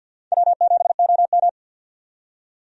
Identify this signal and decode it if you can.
Morse code, heard as a keyed tone, about 52 wpm, 700 Hz. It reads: W7OM
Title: USA Call Signs - 4 Characters - 50wpm